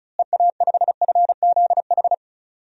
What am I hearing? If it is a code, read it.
Morse code, heard as a keyed tone, about 36 wpm, 700 Hz. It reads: EA5FZH
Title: DXCC-100 Call Signs - 6 Characters - Repeated 2x - 35wpm